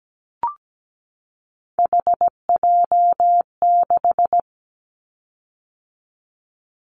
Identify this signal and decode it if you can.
Morse code, heard as a keyed tone, about 17 wpm, 700 Hz. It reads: HJ6
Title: Callsign Prefixes for Top 100 DXCC - 17wpm